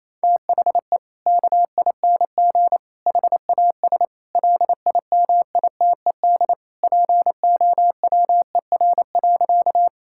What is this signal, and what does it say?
Morse code, heard as a keyed tone, about 28 wpm, 700 Hz. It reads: THE KING HAS LIMITED POWER.
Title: Sentences from Top 1000 Words - Review Part 6 - 28wpm